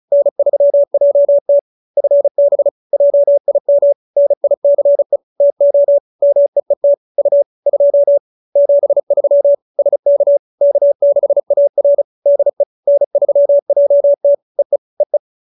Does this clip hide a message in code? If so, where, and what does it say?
Yes: Morse code from 0.1 to 15.2 s, heard as a keyed tone, about 37 wpm, 570 Hz. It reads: N3JT FB JIM NICE TO MEET U 2 73 SK K6AR DE N3JT EE EE